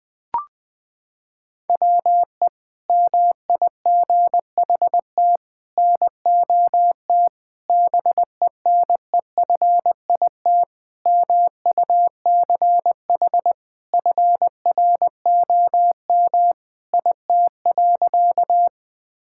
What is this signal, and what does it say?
Morse code, heard as a keyed tone, about 20 wpm, 700 Hz. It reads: WE MIGHT NOT BENEFIT MUCH FROM IT.
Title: Sentences from Top 1000 Words - Review Part 6 - 20wpm